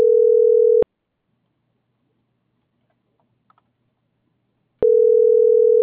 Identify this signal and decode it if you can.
Ringback tone (North America), 440 + 480 Hz.